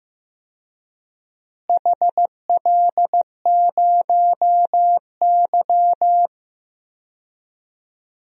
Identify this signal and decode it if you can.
Morse code, heard as a keyed tone, about 15 wpm, 700 Hz. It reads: HL0Y
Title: DXCC-100 Call Signs - 4 Characters - Repeated 2x - 15wpm